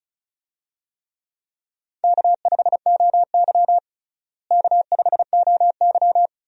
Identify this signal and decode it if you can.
Morse code, heard as a keyed tone, about 36 wpm, 700 Hz. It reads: K5OY K5OY
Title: USA Call Signs - 4 Characters - Repeated 2x - 35wpm